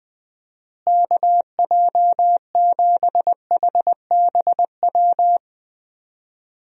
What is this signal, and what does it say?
Morse code, heard as a keyed tone, about 19 wpm, 700 Hz. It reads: KJ7HBW
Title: USA Call Signs - 6 Characters - Repeated 3x - 20wpm